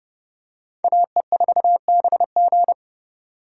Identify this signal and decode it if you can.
Morse code, heard as a keyed tone, about 29 wpm, 700 Hz. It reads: AE4BG